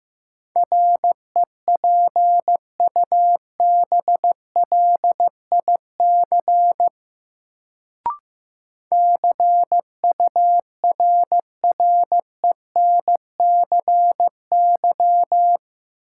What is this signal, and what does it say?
Morse code, heard as a keyed tone, about 15 wpm, 700 Hz. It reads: REPUBLIC CURRENCY
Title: Sentences from Top 1500 Words - New Word Drill Alt - 15wpm